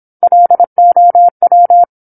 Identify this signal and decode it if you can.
Morse code, heard as a keyed tone, about 27 wpm, 700 Hz. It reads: LOW
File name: 08-Chapter-08 26 WPM